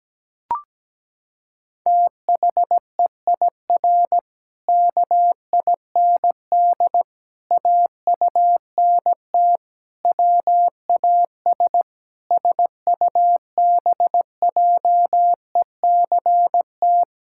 Morse code, heard as a keyed tone, about 17 wpm, 700 Hz. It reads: THEIR KIND AUNT WAS SUBJECT
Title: Pangram Challenge v2 - 17wpm